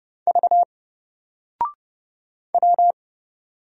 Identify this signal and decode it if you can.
Morse code, heard as a keyed tone, about 29 wpm, 700 Hz. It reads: V W
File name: Single-Letter-Number-30wpm